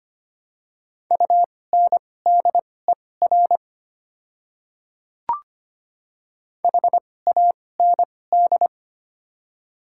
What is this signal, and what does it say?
Morse code, heard as a keyed tone, about 25 wpm, spaced as for 18 wpm, 700 Hz. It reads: UNDER HAND
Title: 2x Character Spacing - Top 200 Words - 25wpm